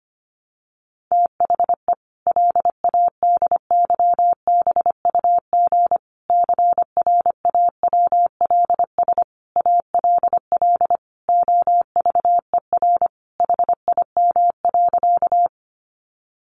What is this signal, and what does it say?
Morse code, heard as a keyed tone, about 25 wpm, 700 Hz. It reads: THE LADYBUG CRAWLS ALL OVER HIM.
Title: Additional Easy Phrases 25wpm